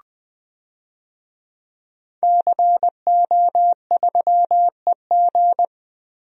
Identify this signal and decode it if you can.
Morse code, heard as a keyed tone, about 20 wpm, 700 Hz. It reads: CO3EG
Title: DXCC-100 Call Signs - 5 Characters - Repeated 2x - 20wpm